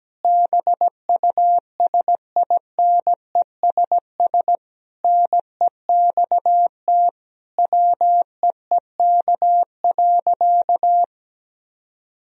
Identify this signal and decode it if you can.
Morse code, heard as a keyed tone, about 17 wpm, 700 Hz. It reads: BUSINESS NEXT WEEK.
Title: Sentences from Top 1000 Words - Review Part 4 - 17wpm